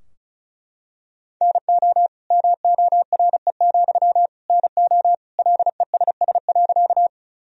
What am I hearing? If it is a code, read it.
Morse code, heard as a keyed tone, about 34 wpm, 700 Hz. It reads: NO MORE, NO LESS.
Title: Sentences from Top 1000 Words - Review Part 2 - 35wpm